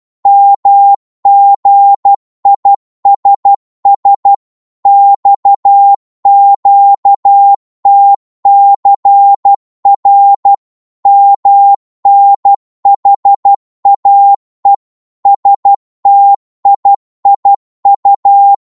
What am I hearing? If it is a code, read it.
Morse code, heard as a keyed tone, about 12 wpm, 800 Hz. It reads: MGISS XQTCR MNHAE STIIU
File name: Morse / ltr-12wpm